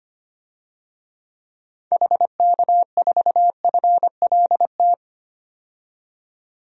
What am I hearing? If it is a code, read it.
Morse code, heard as a keyed tone, about 25 wpm, 700 Hz. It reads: HK4FLT
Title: DXCC-100 Call Signs - 6 Characters - 25wpm